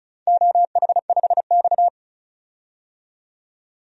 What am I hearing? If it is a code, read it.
Morse code, heard as a keyed tone, about 36 wpm, 700 Hz. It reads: OH5X
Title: DXCC-100 Call Signs - 4 Characters - 35wpm